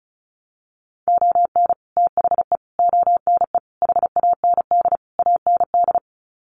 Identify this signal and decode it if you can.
Morse code, heard as a keyed tone, about 34 wpm, 700 Hz. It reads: ON THE ONE HAND AND
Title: Sets of 5 Words Encore2 35wpm